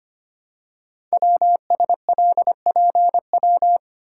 Morse code, heard as a keyed tone, about 25 wpm, 700 Hz. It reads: WSLPW
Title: Code Groups - 5 Letters - 25wpm